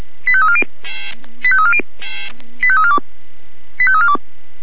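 Five-tone selective call (ZVEI), heard as quick runs of five tones, tones of about 70 ms. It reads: 85340 85340 85342 85342
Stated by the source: ZVEI five-tone selective call 85342 received by SDR.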